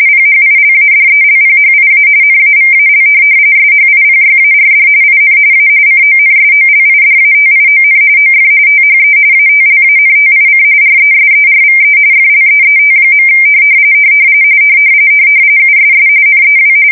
RTTY, about 45.45 baud. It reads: PHILADELPHIA NAVAL SHIPYARD IN A CONSOLIDATION OF THE ATLANTIC RESERVE FLEET. USS NEW
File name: fsk170 / USS-New-Jersey